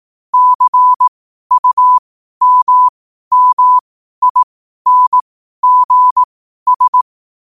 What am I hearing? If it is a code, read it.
Morse code, heard as a keyed tone, about 18 wpm, spaced as for 12 wpm, 1000 Hz. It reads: CUMMINGS
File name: qotdfiles / 12wpm